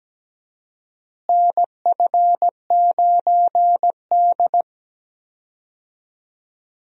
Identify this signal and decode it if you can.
Morse code, heard as a keyed tone, about 17 wpm, 700 Hz. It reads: NF9D
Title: USA Call Signs - 4 Characters - Repeated 2x - 17wpm